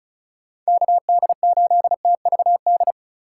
Morse code, heard as a keyed tone, about 34 wpm, 700 Hz. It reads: KD8TVD
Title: USA Call Signs - 6 Characters - 4x Character Spacing - 35wpm